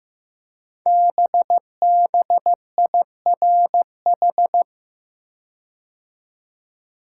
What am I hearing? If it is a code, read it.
Morse code, heard as a keyed tone, about 15 wpm, 700 Hz. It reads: BBIRH
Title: Code Groups - 5 Letters - 15wpm